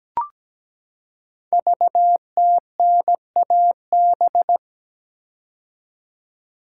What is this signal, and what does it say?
Morse code, heard as a keyed tone, about 17 wpm, 700 Hz. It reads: VTNAB
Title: Code Groups - 5 Letters - 17wpm